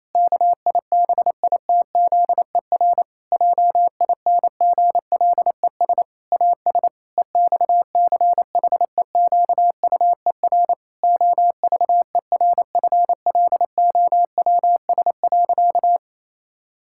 Morse code, heard as a keyed tone, about 28 wpm, 700 Hz. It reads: KIBITZER JINGLES AS EXCHEQUER OVERFLOWS.
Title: Pangram Challenge v2 - 28wpm